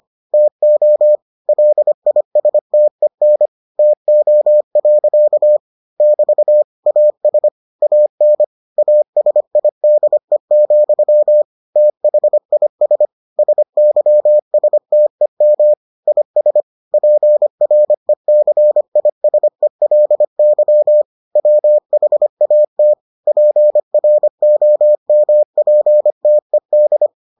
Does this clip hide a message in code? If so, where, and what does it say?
Yes: Morse code from 0.3 to 27.1 s, heard as a keyed tone, about 25 wpm, 600 Hz. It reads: TO LISTEN TO. <BT> AS AN ASIDE, THIS SYSTEM IS PRECISELY WHAT PROMPTED